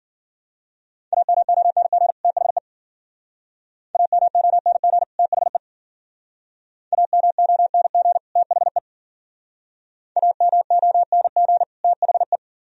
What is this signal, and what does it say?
Morse code, heard as a keyed tone, about 56 wpm, 700 Hz. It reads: AMONG THE AMONG THE AMONG THE AMONG THE